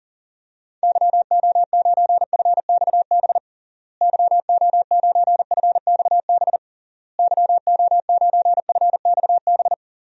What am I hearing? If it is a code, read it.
Morse code, heard as a keyed tone, about 40 wpm, 700 Hz. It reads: YO9FXB YO9FXB YO9FXB
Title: DXCC-100 Call Signs - 6 Characters - Repeated 3x - 40wpm